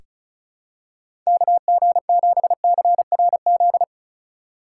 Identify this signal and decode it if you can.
Morse code, heard as a keyed tone, about 36 wpm, 700 Hz. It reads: KG7CRZ
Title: USA Call Signs - 6 Characters - 2x Character Spacing - 35wpm